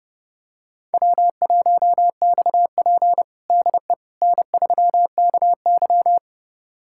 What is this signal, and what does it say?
Morse code, heard as a keyed tone, about 30 wpm, 700 Hz. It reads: W1XP DE N3KY
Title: USA Callsign DE Callsign - 4 Characters - 30wpm